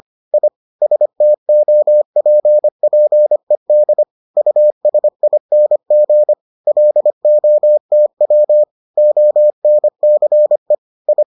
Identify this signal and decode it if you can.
Morse code, heard as a keyed tone, about 25 wpm, 600 Hz. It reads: I STOPPED USING LOTW ONCE I